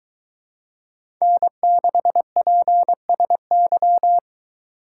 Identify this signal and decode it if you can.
Morse code, heard as a keyed tone, about 23 wpm, 700 Hz. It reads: N6PSY